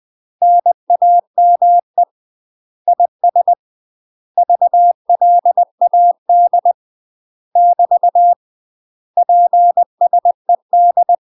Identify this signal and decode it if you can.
Morse code, heard as a keyed tone, about 20 wpm, 700 Hz. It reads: NAME IS VLAD <BT> PSED
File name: UK MASTER QSOs TRAINING / 20WPM / UK MASTER QSOs 0032